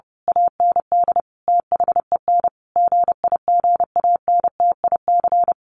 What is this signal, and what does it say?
Morse code, heard as a keyed tone, about 30 wpm, 700 Hz. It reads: AND THEN GIGANTIC
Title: A Sound Of Thunder 30wpm